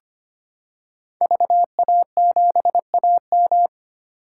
Morse code, heard as a keyed tone, about 25 wpm, 700 Hz. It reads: VA7AM
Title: K1USN SST QSO Practice - 25wpm